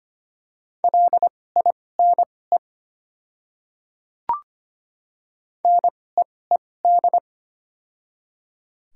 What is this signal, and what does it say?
Morse code, heard as a keyed tone, about 25 wpm, spaced as for 18 wpm, 700 Hz. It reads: LINE NEED
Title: 2x Character Spacing - Top 200 Words - 25wpm